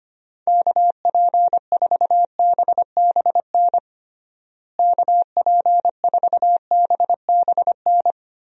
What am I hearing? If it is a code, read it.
Morse code, heard as a keyed tone, about 25 wpm, 700 Hz. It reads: KP4BBN KP4BBN
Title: DXCC-100 Call Signs - 6 Characters - Repeated 2x - 25wpm